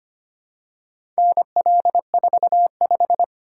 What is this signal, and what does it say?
Morse code, heard as a keyed tone, about 25 wpm, 700 Hz. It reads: NL45
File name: practice / 25wpm / Four Letter-Number Combinations - 25wpm